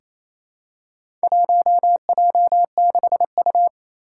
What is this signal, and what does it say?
Morse code, heard as a keyed tone, about 27 wpm, 700 Hz. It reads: 1J6U
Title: Four Challenging Letter-Number Combinations 28wpm